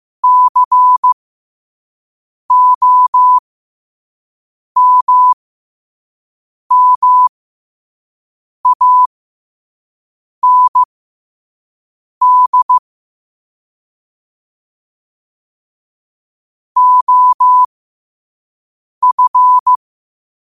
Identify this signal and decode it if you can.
Morse code, heard as a keyed tone, about 15 wpm, spaced as for 5 wpm, 1000 Hz. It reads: COMMAND OF
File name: qotdfiles / 5wpm